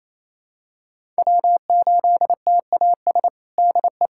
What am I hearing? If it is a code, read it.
Morse code, heard as a keyed tone, about 28 wpm, 700 Hz. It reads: W8TAS DE
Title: USA Callsign DE Callsign 28wpm